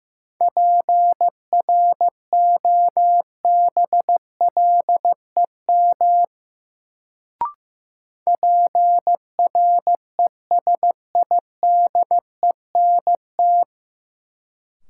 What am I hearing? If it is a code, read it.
Morse code, heard as a keyed tone, about 15 wpm, 700 Hz. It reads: PROBLEM PRESIDENT
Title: Sentences from Top 1000 Words - New Word Drill - 15wpm